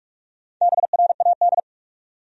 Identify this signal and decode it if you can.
Morse code, heard as a keyed tone, about 45 wpm, 700 Hz. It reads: BRAD
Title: USA Common Names - Max 5 Letters - 45wpm